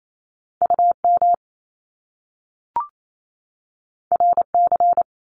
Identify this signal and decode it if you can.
Morse code, heard as a keyed tone, about 28 wpm, 700 Hz. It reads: UM RC
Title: Top Two Letter Combinations in Words 28wpm